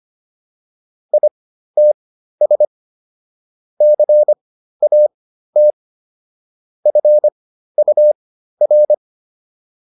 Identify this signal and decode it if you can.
Morse code, heard as a keyed tone, about 25 wpm, spaced as for 13 wpm, 600 Hz. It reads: ITS CAT FUR